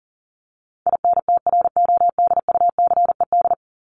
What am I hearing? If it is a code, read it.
Morse code, heard as a keyed tone, about 40 wpm, 700 Hz. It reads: INTRODUCED
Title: ED Suffix - Top 100 Words with Base Words 40wpm